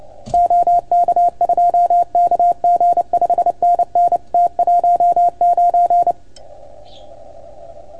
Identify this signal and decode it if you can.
Morse code, heard as a keyed tone, about 30 wpm, 690 Hz. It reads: OK2KG 5NN T19
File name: Archiv / CUC310 / OK1FGD CUC 310 pileup